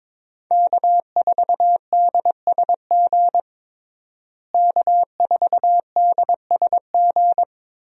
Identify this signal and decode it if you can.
Morse code, heard as a keyed tone, about 22 wpm, 700 Hz. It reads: K4DSG K4DSG
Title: USA Call Signs - 5 Characters - Repeated 2x - 22wpm